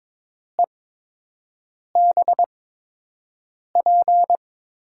Morse code, heard as a keyed tone, about 22 wpm, spaced as for 6 wpm, 700 Hz. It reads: EBP
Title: USA Call Signs - 6 Characters - 8x Character Spacing - 22wpm